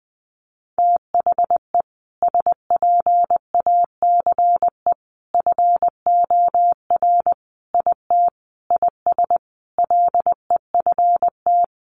Morse code, heard as a keyed tone, about 20 wpm, 700 Hz. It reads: THE SPACE FOR IT IS LEFT
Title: Sentences from Top 1500 Words 20wpm